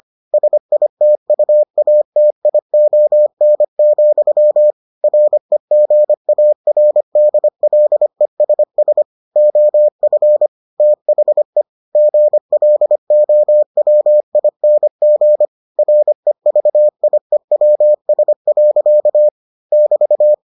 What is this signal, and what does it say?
Morse code, heard as a keyed tone, about 25 wpm, 600 Hz. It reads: SITUATION, REGARDLESS OF THE GLOWING REVIEWS. <BT>